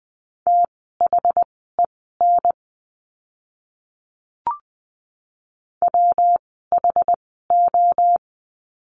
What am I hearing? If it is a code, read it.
Morse code, heard as a keyed tone, about 20 wpm, spaced as for 15 wpm, 700 Hz. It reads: THEN WHO
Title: 192 Lesson 31f - Letter v - Words 20wpm